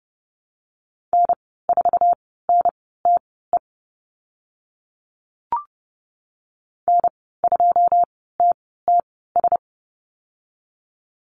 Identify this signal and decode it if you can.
Morse code, heard as a keyed tone, about 32 wpm, spaced as for 17 wpm, 700 Hz. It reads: N4NTE N2TTS
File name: Lesson 15g - Number 5 - Call Signs 30wpm